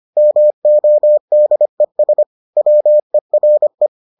Morse code, heard as a keyed tone, about 25 wpm, 600 Hz. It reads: MODES WERE